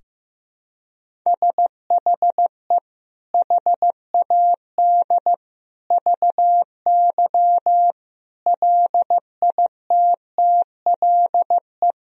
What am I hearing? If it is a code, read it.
Morse code, heard as a keyed tone, about 15 wpm, 700 Hz. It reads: SHE HAD VY LITTLE
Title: Sentences from Top 400 Words and Abbreviations 15wpm